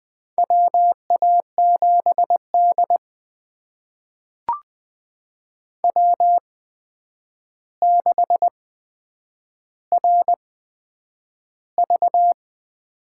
Morse code, heard as a keyed tone, about 20 wpm, spaced as for 9 wpm, 700 Hz. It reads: WA7D W6RV
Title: USA Call Signs - 4 Characters - 8x Character Spacing - 20wpm